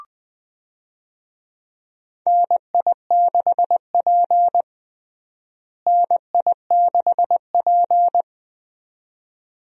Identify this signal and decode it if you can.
Morse code, heard as a keyed tone, about 20 wpm, 700 Hz. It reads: NI6P NI6P
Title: USA Call Signs - 4 Characters - Repeated 2x - 20wpm